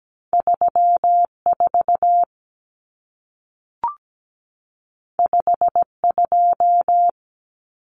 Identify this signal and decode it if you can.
Morse code, heard as a keyed tone, about 17 wpm, 700 Hz. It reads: 34 52
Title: Numbers - 2-Digit 17wpm